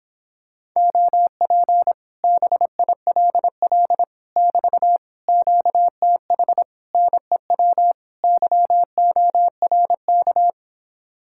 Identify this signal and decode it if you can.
Morse code, heard as a keyed tone, about 26 wpm, 700 Hz. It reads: OP BILL <BT> QTH NEW YORK